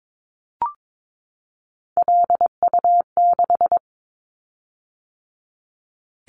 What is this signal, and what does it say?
Morse code, heard as a keyed tone, about 22 wpm, 700 Hz. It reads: LU6
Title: Callsign Prefixes for Top 100 DXCC - 22wpm